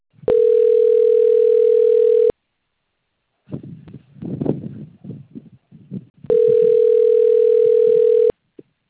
Ringback tone (North America), 440 + 480 Hz.
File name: out- - - - - .29819